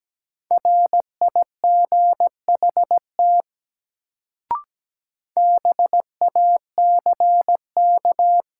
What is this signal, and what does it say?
Morse code, heard as a keyed tone, about 17 wpm, 700 Hz. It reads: RIGHT BACK
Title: Sentences from Top 200 Words - New Word Drill - 17wpm